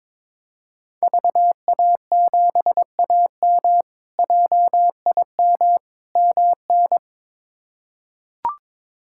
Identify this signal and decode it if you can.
Morse code, heard as a keyed tone, about 22 wpm, 700 Hz. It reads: VA7AM JIM MN
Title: K1USN SST QSO Practice - 22wpm